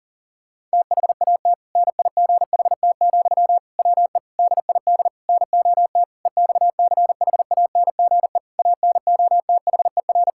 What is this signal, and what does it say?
Morse code, heard as a keyed tone, about 38 wpm, 700 Hz. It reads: THAT NIGHT, WE DID NOT EXCHANGE ANOTHER
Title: Sentences from Top 1000 Words - Review Part 6 - 40wpm